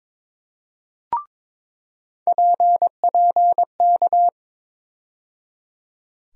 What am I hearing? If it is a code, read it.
Morse code, heard as a keyed tone, about 22 wpm, 700 Hz. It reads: PPK
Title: USA Call Sign Postfixes - 3 Letters 22wpm